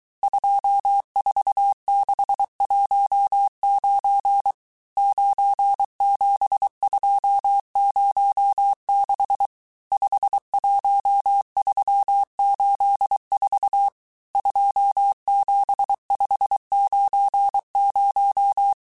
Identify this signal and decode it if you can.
Morse code, heard as a keyed tone, about 23 wpm, 800 Hz. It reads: 24619 97206 51384 27590
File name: cours-du-mercredi / audios / cours-855-texte-chiffre-vitesse-22-mpm